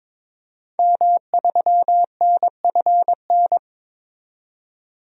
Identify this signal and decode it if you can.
Morse code, heard as a keyed tone, about 21 wpm, 700 Hz. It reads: M3NFN
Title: DXCC-100 Call Signs - 5 Characters - 22wpm